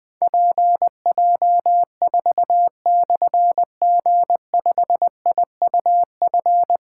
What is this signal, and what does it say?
Morse code, heard as a keyed tone, about 20 wpm, 700 Hz. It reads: PJ4/G5IUF
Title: Difficult Call Signs 20wpm